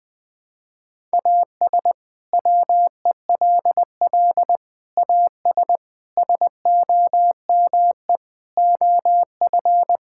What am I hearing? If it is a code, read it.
Morse code, heard as a keyed tone, about 20 wpm, 700 Hz. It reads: AS WELL AS SOME OF
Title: Sets of 6 Words - 20wpm